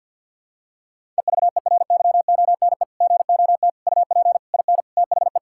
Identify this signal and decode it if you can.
Morse code, heard as a keyed tone, about 50 wpm, 700 Hz. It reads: EVERYONE GOT UP IN THE
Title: Sentences from Top 1000 Words - Review Part 5 - 50wpm